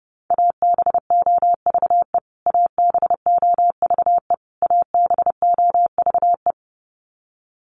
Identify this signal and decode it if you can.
Morse code, heard as a keyed tone, about 30 wpm, 700 Hz. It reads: ABOVE ABOVE ABOVE
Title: Top 500 Words Repeated 30wpm